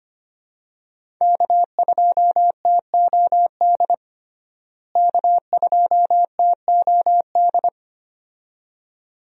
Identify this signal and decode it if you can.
Morse code, heard as a keyed tone, about 25 wpm, 700 Hz. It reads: K2TOD K2TOD
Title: USA Call Signs - 5 Characters - Repeated 2x - 25wpm